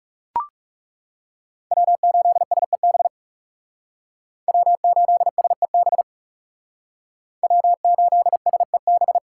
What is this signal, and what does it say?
Morse code, heard as a keyed tone, about 36 wpm, 700 Hz. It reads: W8SEB W8SEB W8SEB
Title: Speed Racing - USA 1x3 Call Signs 30wpm